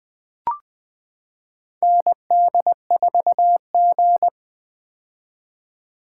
Morse code, heard as a keyed tone, about 20 wpm, 700 Hz. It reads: ND4G
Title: K1USN SST QSO Practice - 20wpm